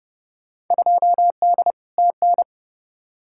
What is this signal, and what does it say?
Morse code, heard as a keyed tone, about 29 wpm, 700 Hz. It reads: 2D TN
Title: Field Day Exchanges Encore - 30wpm